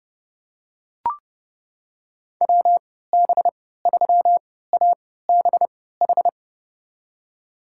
Morse code, heard as a keyed tone, about 30 wpm, spaced as for 17 wpm, 700 Hz. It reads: WB3ABH
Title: USA Call Signs - 6 Characters - 3x Character Spacing - 30wpm